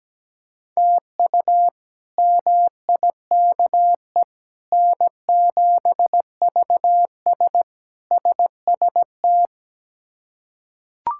Morse code, heard as a keyed tone, about 17 wpm, 700 Hz. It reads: TU MIKE N7VS SST
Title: K1USN SST QSO Practice - 17wpm